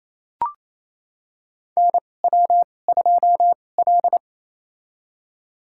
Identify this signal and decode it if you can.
Morse code, heard as a keyed tone, about 28 wpm, spaced as for 20 wpm, 690 Hz. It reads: NW2L
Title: USA Call Signs - 2x Character Spacing - 28wpm